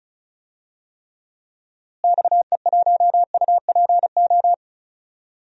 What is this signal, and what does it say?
Morse code, heard as a keyed tone, about 36 wpm, 700 Hz. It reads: XE1UPO
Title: DXCC-100 Call Signs - 6 Characters - Repeated 3x - 35wpm